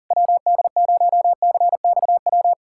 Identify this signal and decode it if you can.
Morse code, heard as a keyed tone, about 40 wpm, 700 Hz. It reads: WD0CXW
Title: USA Callsign DE Callsign - 6 Characters - 40wpm